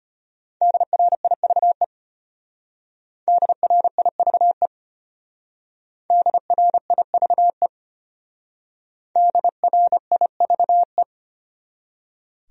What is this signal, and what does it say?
Morse code, heard as a keyed tone, about 31 wpm, 700 Hz. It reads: DRIVE DRIVE DRIVE DRIVE
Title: Speed-Racing Top 500 Words 25wpm